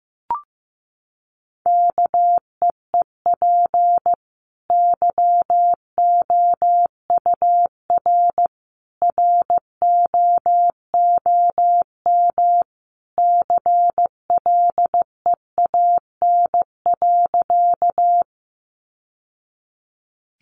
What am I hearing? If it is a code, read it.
Morse code, heard as a keyed tone, about 15 wpm, 700 Hz. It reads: KEEP YOUR ROOM CLEAN.
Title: Sentences from Top 1100 Words 15wpm